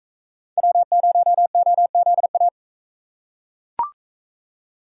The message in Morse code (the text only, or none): W0OZA